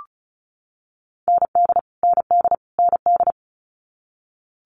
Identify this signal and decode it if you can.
Morse code, heard as a keyed tone, about 36 wpm, 700 Hz. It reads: ND ND ND
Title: Multi-letter Phonemes Repeated 3x 35wpm